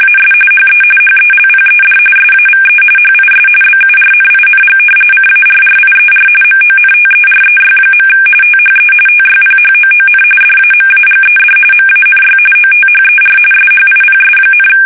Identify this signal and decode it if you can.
RTTY, about 45.45 baud. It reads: AS HE HELPED SAVE HIS BURNING CARRIER MONTEREY. THE TYPHOON DAMAGED MANY OTHER SHIPS,
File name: fsk850 / battleshipnj